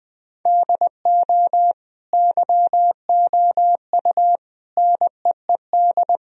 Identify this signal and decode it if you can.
Morse code, heard as a keyed tone, about 20 wpm, 700 Hz. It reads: DO YOU NEED
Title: Sentences from Top 1000 Words - Review Part 4 - 20wpm